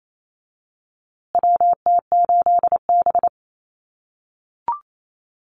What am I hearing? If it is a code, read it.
Morse code, heard as a keyed tone, about 28 wpm, 700 Hz. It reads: WT8B
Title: USA Call Signs Repeated 3x 28wpm